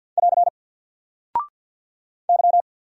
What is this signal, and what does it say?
Morse code, heard as a keyed tone, about 48 wpm, 700 Hz. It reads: <AR> Y
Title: Single Character (Letter:Number:Punctuation:Prosign) Alt 50wpm